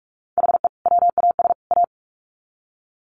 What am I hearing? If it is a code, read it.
Morse code, heard as a keyed tone, about 44 wpm, 700 Hz. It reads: HE WAS A
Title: Sets of 3 Words 45wpm